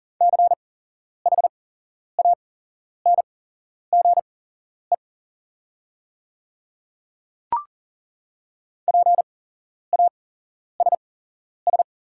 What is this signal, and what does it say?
Morse code, heard as a keyed tone, about 40 wpm, spaced as for 11 wpm, 690 Hz. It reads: CHANGE PASS
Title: 8x Character Spacing - Top 500 Words - 40wpm